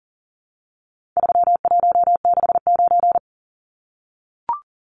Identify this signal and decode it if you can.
Morse code, heard as a keyed tone, about 40 wpm, 700 Hz. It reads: 3169
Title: Numbers - 4-Digit 40wpm